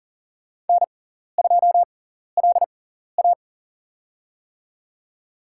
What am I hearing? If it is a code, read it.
Morse code, heard as a keyed tone, about 39 wpm, spaced as for 14 wpm, 700 Hz. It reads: N2LA
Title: USA Call Signs - 6x Character Spacing - 40wpm